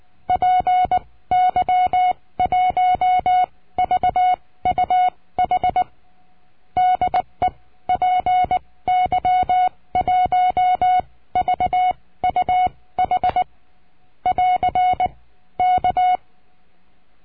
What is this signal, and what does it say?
Morse code, heard as a keyed tone, about 19 wpm, spaced as for 16 wpm, 730 Hz. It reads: PY1VUH DE PY1VUH <AR> K
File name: cursos / CURSO DE TELEGRAFIA / Telegrafia - Trilha 22